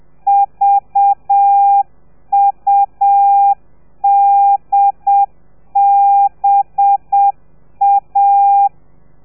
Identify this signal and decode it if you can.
Morse code, heard as a keyed tone, about 7 wpm, 800 Hz. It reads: VUDBA
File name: telegrafia / i2tfj / Lez4 30